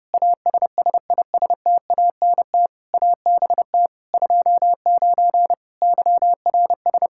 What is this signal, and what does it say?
Morse code, heard as a keyed tone, about 30 wpm, 700 Hz. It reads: ASSISTANT ABT 29 YRS
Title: Practice QSO - Exchange 3 - More Info - 30wpm